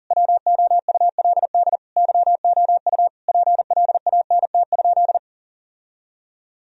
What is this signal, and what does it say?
Morse code, heard as a keyed tone, about 40 wpm, 700 Hz. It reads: WOULD YOU PLANT?
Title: Sentences from Top 1000 Words - Review Part 1 - 40wpm